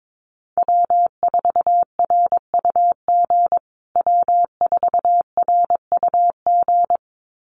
Morse code, heard as a keyed tone, about 22 wpm, 700 Hz. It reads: W4RUG W4RUG
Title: USA Call Signs Repeated 2x 22wpm